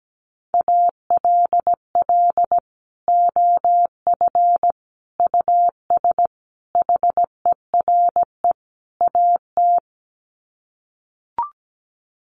Morse code, heard as a keyed tone, about 17 wpm, 700 Hz. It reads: ALL OF US HERE AT
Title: Sets of 5 Words 17wpm